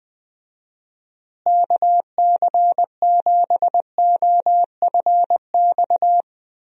Morse code, heard as a keyed tone, about 21 wpm, 700 Hz. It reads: KC7OFX
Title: USA Call Signs - 6 Characters - Repeated 3x - 20wpm